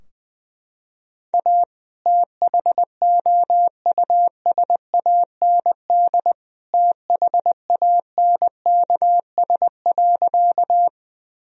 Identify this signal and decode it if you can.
Morse code, heard as a keyed tone, about 20 wpm, 700 Hz. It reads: A THOUSAND THANKS.
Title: Sentences from Top 1000 Words - Review Part 3 - 20wpm